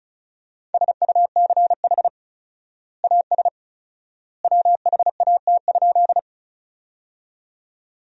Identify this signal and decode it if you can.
Morse code, heard as a keyed tone, about 34 wpm, 700 Hz. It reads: SUCH AS WHAT?
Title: 4x Word-Spacing - Sentences from Top 200 Words - 35wpm